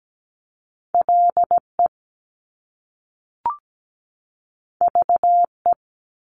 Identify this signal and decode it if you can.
Morse code, heard as a keyed tone, about 17 wpm, 700 Hz. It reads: LE VE
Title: Top Two Letter Combinations in Words 17wpm